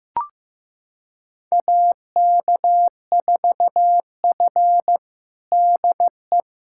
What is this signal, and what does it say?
Morse code, heard as a keyed tone, about 15 wpm, 700 Hz. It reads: AK4F DE
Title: USA Callsign DE Callsign - 4 Characters - 15wpm